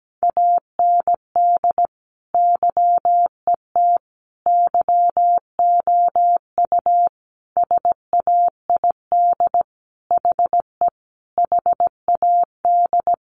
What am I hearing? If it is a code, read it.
Morse code, heard as a keyed tone, about 17 wpm, 700 Hz. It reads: AND YET YOU SAID HE HAD
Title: Sentences from Top 500 Words 17wpm